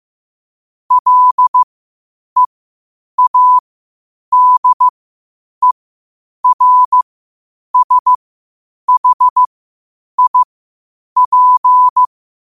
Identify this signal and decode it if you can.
Morse code, heard as a keyed tone, about 15 wpm, spaced as for 8 wpm, 1000 Hz. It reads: LEADERSHIP
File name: qotdfiles / 8wpm